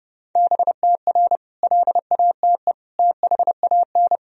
Morse code, heard as a keyed tone, about 30 wpm, 700 Hz. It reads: BTR LATE THAN
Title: Sentences from Top 400 Words and Abbreviations 30wpm